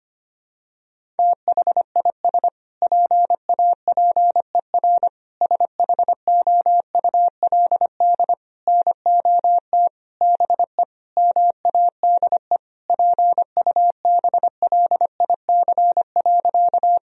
Morse code, heard as a keyed tone, about 25 wpm, 700 Hz. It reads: THIS PAPER SHOULD NOT BE MADE PUBLIC.
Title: Sentences from Top 1000 Words - Review Part 3 - 25wpm